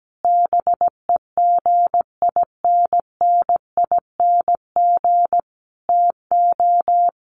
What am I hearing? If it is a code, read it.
Morse code, heard as a keyed tone, about 17 wpm, 700 Hz. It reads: BEGINNING TO
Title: Sentences from Top 1500 Words 17wpm